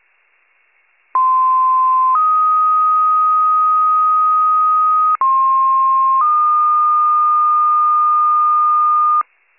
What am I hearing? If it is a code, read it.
Two-tone page, heard as a one-second tone, then a three-second tone, about 1040 then 1250 Hz, about 1060 then 1220 Hz. It is heard from 1.1 to 9.2 s.